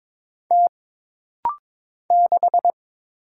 Morse code, heard as a keyed tone, about 22 wpm, 700 Hz. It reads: T 6
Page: This rapid-fire timing is geared to push your ICR (Instant Character Recognition) skills.